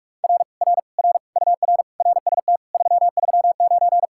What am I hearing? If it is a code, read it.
Morse code, heard as a keyed tone, about 57 wpm, 700 Hz. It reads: R R R UR RST 339